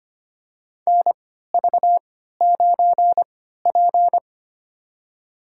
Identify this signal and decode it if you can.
Morse code, heard as a keyed tone, about 25 wpm, spaced as for 14 wpm, 700 Hz. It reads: NV9P
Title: USA Call Signs - 3x Character Spacing - 25wpm